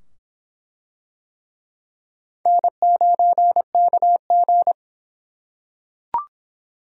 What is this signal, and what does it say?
Morse code, heard as a keyed tone, about 23 wpm, 700 Hz. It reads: N9KG